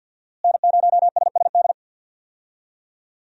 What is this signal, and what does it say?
Morse code, heard as a keyed tone, about 50 wpm, 700 Hz. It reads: N0SSD
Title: USA Call Signs - 5 Characters - Repeated 3x - 50wpm